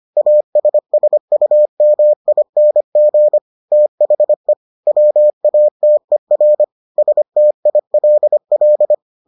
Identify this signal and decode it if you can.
Morse code, heard as a keyed tone, about 25 wpm, 600 Hz. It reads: ASSUMING THE WATER STILL